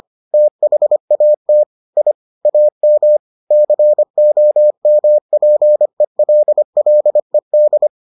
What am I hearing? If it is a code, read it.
Morse code, heard as a keyed tone, about 25 wpm, 600 Hz. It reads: THAT I AM COMPELLED